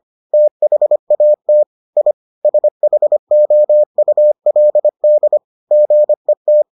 Morse code, heard as a keyed tone, about 25 wpm, 600 Hz. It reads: THAT I SHOULD GET